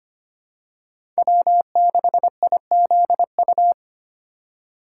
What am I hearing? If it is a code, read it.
Morse code, heard as a keyed tone, about 25 wpm, 700 Hz. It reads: W6IZU
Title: USA Call Signs - 5 Characters - 4x Character Spacing - 25wpm